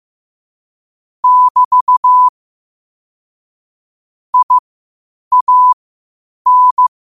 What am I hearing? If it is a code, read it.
Morse code, heard as a keyed tone, about 15 wpm, spaced as for 8 wpm, 1000 Hz. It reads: <BT> IAN
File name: qotdfiles / 8wpm